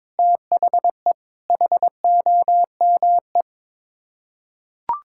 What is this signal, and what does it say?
Morse code, heard as a keyed tone, about 22 wpm, 700 Hz. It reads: THE HOME
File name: practice / 22wpm / Sets of 8 Words - 22wpm